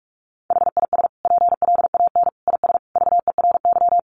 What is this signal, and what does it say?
Morse code, heard as a keyed tone, about 44 wpm, 700 Hz. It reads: HIS PLAN IS VERY
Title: Sentences from Top 500 Words 45wpm